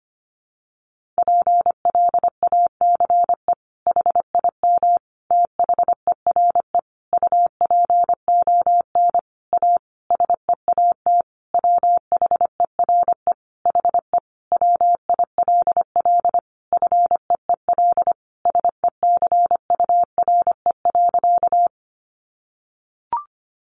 Morse code, heard as a keyed tone, about 25 wpm, 700 Hz. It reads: PLACE HIM THERE UPON A SEAT WHERE HE WILL FEEL SECURE.
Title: Sentences from Top 1500 Words 25wpm